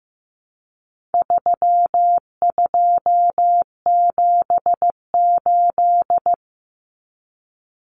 Morse code, heard as a keyed tone, about 15 wpm, 700 Hz. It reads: 3278
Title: Common Cut Numbers - 4-Digit 15wpm